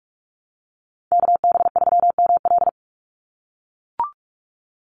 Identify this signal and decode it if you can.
Morse code, heard as a keyed tone, about 44 wpm, 700 Hz. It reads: KB3ML
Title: Call Signs - 45wpm